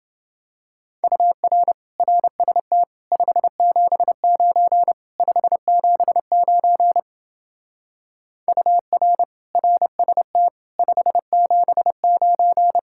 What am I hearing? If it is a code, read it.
Morse code, heard as a keyed tone, about 29 wpm, 700 Hz. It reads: UR RST 579 579 UR RST 579